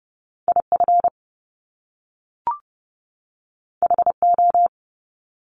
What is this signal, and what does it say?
Morse code, heard as a keyed tone, about 30 wpm, 700 Hz. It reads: IF HO
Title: Top Two Letter Combinations in Words 30wpm